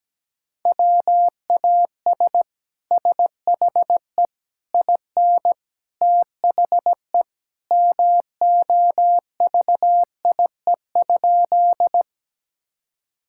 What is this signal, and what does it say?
Morse code, heard as a keyed tone, about 17 wpm, 700 Hz. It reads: WAS SHE IN THE MOVIE?
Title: Sentences from Top 1000 Words - Review Part 5 - 17wpm